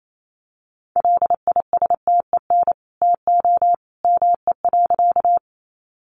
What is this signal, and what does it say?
Morse code, heard as a keyed tone, about 28 wpm, 700 Hz. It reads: LISTEN TO ME.
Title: Sentences from Top 400 Words 28wpm